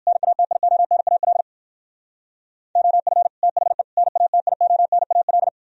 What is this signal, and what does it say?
Morse code, heard as a keyed tone, about 58 wpm, 700 Hz. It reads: NATIONAL OF THE NATIONAL